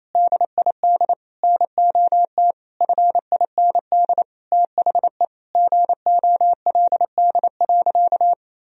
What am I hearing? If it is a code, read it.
Morse code, heard as a keyed tone, about 28 wpm, 700 Hz. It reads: DID NOT FIND THE GOLD.
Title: Sentences from Top 1000 Words - Review Part 3 - 28wpm